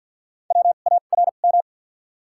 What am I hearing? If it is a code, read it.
Morse code, heard as a keyed tone, about 52 wpm, spaced as for 37 wpm, 700 Hz. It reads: WARM
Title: 2x Character Spacing - Top 500 Words - 50wpm